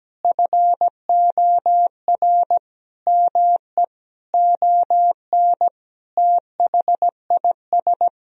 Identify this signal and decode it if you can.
Morse code, heard as a keyed tone, about 17 wpm, 700 Hz. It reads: FOR ME ON THIS
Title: Sentences from Top 1000 Words - Review Part 2 - 17wpm